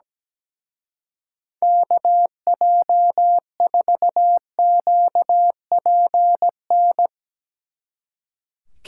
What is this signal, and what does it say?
Morse code, heard as a keyed tone, about 17 wpm, 700 Hz. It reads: KJ4QPN
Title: Speed Racing - USA 2x3 Call Signs 17wpm